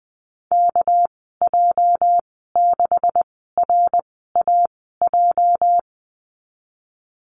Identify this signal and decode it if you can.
Morse code, heard as a keyed tone, about 19 wpm, spaced as for 14 wpm, 700 Hz. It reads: KJ6RAJ
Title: 207 Lesson 33g - Letter j - Call Signs 20wpm